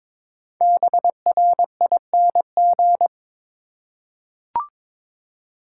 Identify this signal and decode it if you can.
Morse code, heard as a keyed tone, about 22 wpm, 700 Hz. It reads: BRING
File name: 5 Letter Words - First 3 Letters Given - 22wpm